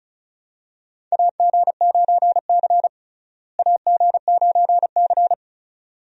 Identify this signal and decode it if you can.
Morse code, heard as a keyed tone, about 34 wpm, 700 Hz. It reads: AG9C AG9C
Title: USA Call Signs - 4 Characters - Repeated 2x - 35wpm